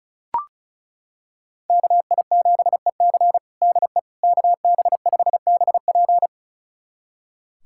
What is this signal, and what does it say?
Morse code, heard as a keyed tone, about 36 wpm, 700 Hz. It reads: KI7EC DE KB5BP
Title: USA Callsign DE Callsign - 5 Characters - 35wpm